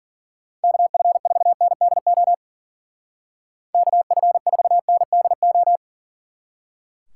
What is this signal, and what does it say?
Morse code, heard as a keyed tone, about 50 wpm, 700 Hz. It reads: KF4NDO KF4NDO
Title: Speed Racing - USA 2x3 Call Signs 40wpm